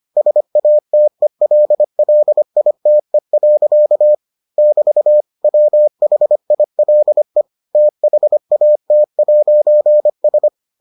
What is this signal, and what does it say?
Morse code, heard as a keyed tone, about 25 wpm, 600 Hz. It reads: SATELLITE. <BT> WHILE THAT'S